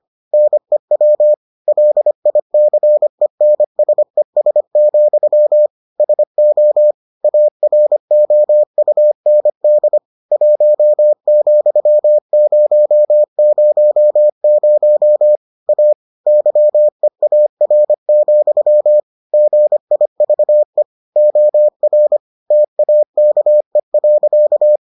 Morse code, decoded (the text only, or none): NEW LICENSES, SO AROUND 1,000 A YEAR, GIVE OR TAKE.